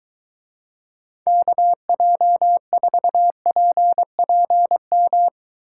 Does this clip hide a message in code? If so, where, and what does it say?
Yes: Morse code from 1.3 to 5.3 s, heard as a keyed tone, about 22 wpm, 700 Hz. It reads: KJ4PPM